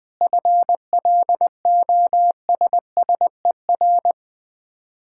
Morse code, heard as a keyed tone, about 20 wpm, 700 Hz. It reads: FLOSSER
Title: Difficult Words - Speed-Racing with Hints - 20wpm